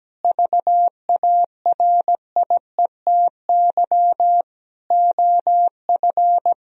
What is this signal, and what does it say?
Morse code, heard as a keyed tone, about 17 wpm, 700 Hz. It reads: VARIETY OF
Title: Sentences from Top 1000 Words - Review Part 4 - 17wpm